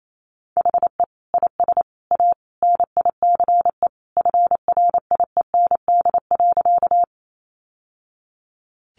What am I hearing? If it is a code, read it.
Morse code, heard as a keyed tone, about 28 wpm, 700 Hz. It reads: HE IS A NICE FRIEND.
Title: Sentences from Top 1100 Words 28wpm